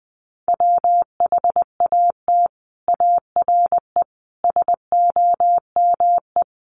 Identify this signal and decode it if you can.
Morse code, heard as a keyed tone, about 20 wpm, 700 Hz. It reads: WHAT ARE SOME
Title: Sets of 5 Words Encore 20wpm